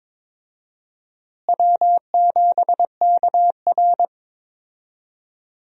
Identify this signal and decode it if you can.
Morse code, heard as a keyed tone, about 23 wpm, 700 Hz. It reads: W7KR
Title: USA Call Signs - 4 Characters - Repeated 2x - 22wpm